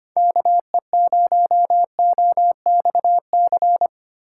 Morse code, heard as a keyed tone, about 25 wpm, 700 Hz. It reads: KE0OXC
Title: USA Call Signs - 3x Character Spacing - 25wpm